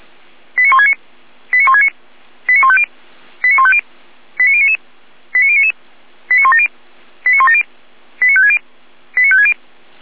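Five-tone selective call (ZVEI), heard as quick runs of five tones, tones of about 70 ms. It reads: 89169 89169 89150 89150 89099 89099 89170 89170 89560 89560
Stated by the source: SDR-received ZVEI five-tone selective call.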